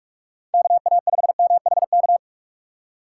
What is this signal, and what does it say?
Morse code, heard as a keyed tone, about 47 wpm, 700 Hz. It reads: KA5MHK
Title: USA Call Signs - 6 Characters - 4x Character Spacing - 45wpm